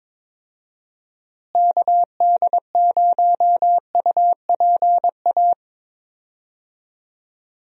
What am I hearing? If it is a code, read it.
Morse code, heard as a keyed tone, about 22 wpm, 700 Hz. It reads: KD0UPA
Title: USA Call Signs - 6 Characters - Repeated 2x - 22wpm